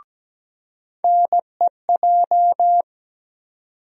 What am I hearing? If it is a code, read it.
Morse code, heard as a keyed tone, about 17 wpm, 700 Hz. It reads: NEJ
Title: USA Call Sign Postfixes - 3 Letters 17wpm